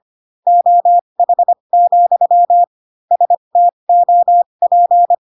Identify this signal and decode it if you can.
Morse code, heard as a keyed tone, about 25 wpm, spaced as for 22 wpm, 700 Hz. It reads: OH, STOP